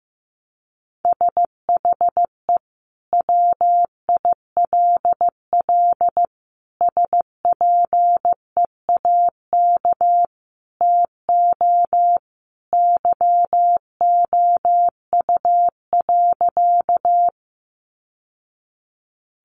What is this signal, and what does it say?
Morse code, heard as a keyed tone, about 15 wpm, 700 Hz. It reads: SHE WILL SPEAK TO YOU.
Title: Sentences from Top 1100 Words 15wpm